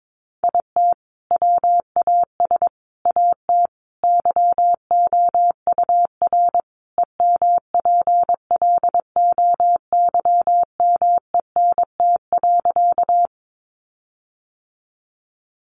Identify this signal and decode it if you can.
Morse code, heard as a keyed tone, about 22 wpm, 700 Hz. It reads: IT WAS AT YOUR EMPLOYMENT.
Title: Sentences from Top 1500 Words 22wpm